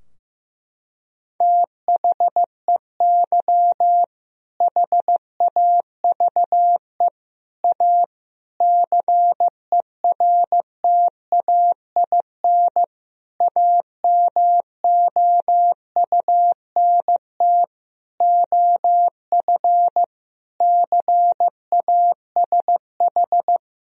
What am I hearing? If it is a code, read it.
Morse code, heard as a keyed tone, about 15 wpm, 700 Hz. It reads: THEY HAVE A CERTAIN AMOUNT OF CASH
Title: Sentences from Top 1000 Words - Review Part 4 - 15wpm